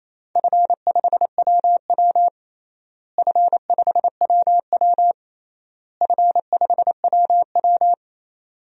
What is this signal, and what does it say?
Morse code, heard as a keyed tone, about 28 wpm, 700 Hz. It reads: F5WW F5WW F5WW
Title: DXCC-100 Call Signs - 4 Characters - Repeated 3x - 28wpm